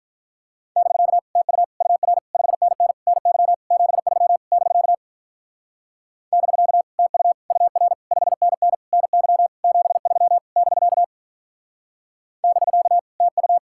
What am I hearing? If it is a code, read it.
Morse code, heard as a keyed tone, about 45 wpm, 700 Hz. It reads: <BK> TU UR 5NN NY 73 <BK> <BK> TU UR 5NN NY 73 <BK> <BK> TU